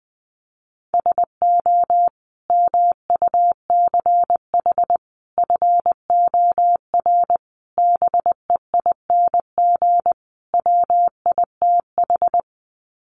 Morse code, heard as a keyed tone, about 20 wpm, 700 Hz. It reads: SO MUCH FOR BEING WITH
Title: Sets of 5 Words Encore 20wpm